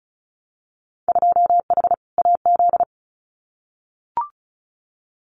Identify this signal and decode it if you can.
Morse code, heard as a keyed tone, about 36 wpm, 700 Hz. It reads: 2H AZ
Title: Winter Field Day Exchanges 35wpm